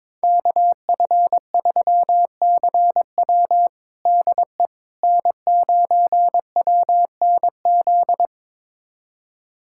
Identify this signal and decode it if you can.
Morse code, heard as a keyed tone, about 22 wpm, 700 Hz. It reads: KF3CW DE N9WNZ
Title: USA Callsign DE Callsign - 5 Characters - 22wpm